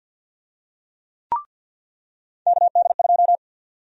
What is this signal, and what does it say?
Morse code, heard as a keyed tone, about 49 wpm, 700 Hz. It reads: KD2
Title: USA Call Sign Prefixes - 50wpm